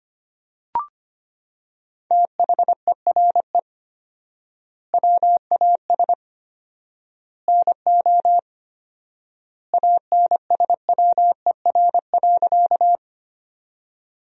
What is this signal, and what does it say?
Morse code, heard as a keyed tone, about 25 wpm, 700 Hz. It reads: THERE WAS NO ANSWER.
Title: 4x Word-Spacing - Sentences from Top 300 Words - 25wpm